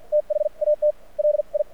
Morse code, heard as a keyed tone, about 50 wpm, spaced as for 40 wpm, 600 Hz, with radio noise. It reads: THAT LI
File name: archived / CRSnet